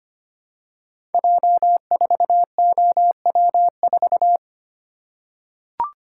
Morse code, heard as a keyed tone, about 25 wpm, 700 Hz. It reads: J4OW4
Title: Code Groups - 5 Letter:Numbers - 25wpm